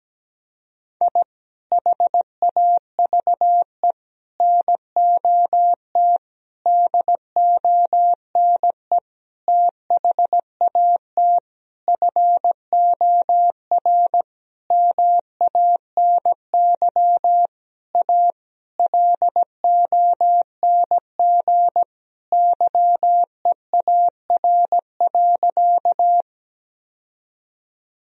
Morse code, heard as a keyed tone, about 17 wpm, 700 Hz. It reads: I HAVE NOT DONE THAT FOR MANY A LONG YEAR.
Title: Sentences from Top 1000 Words - Review Part 1 - 17wpm